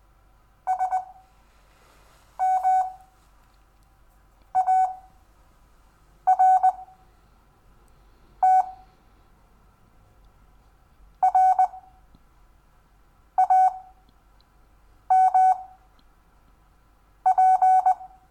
Morse code, heard as a keyed tone, about 20 wpm, spaced as for 5 wpm, 750 Hz. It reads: SMART RAMP
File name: cw / Lesson4 REVIEW